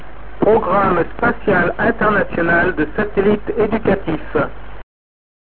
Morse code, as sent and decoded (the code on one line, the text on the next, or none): none